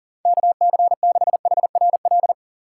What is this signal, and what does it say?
Morse code, heard as a keyed tone, about 37 wpm, 700 Hz. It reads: KC6HRL
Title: USA Call Signs - 6 Characters - 6x Character Spacing - 40wpm